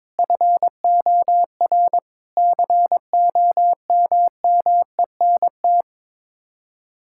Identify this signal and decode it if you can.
Morse code, heard as a keyed tone, about 22 wpm, 700 Hz. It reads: FOR COMMENT
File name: practice / 22wpm / Sets of 7 Words - 22wpm